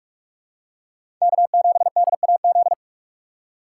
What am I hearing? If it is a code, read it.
Morse code, heard as a keyed tone, about 43 wpm, 700 Hz. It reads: K7DAZ
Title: USA Call Signs - 5 Characters - 6x Character Spacing - 45wpm